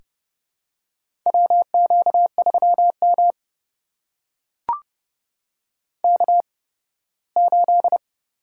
Morse code, heard as a keyed tone, about 32 wpm, spaced as for 18 wpm, 700 Hz. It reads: WQ3M K8
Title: USA Call Signs - 4 Characters - 8x Character Spacing - 30wpm